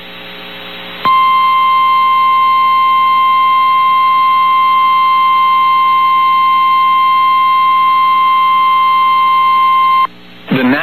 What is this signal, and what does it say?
Emergency Alert System (SAME) tones (the 1050 Hz attention tone). It is heard from 1.0 to 10.1 s.
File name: eas received alerts / L1 19559 audio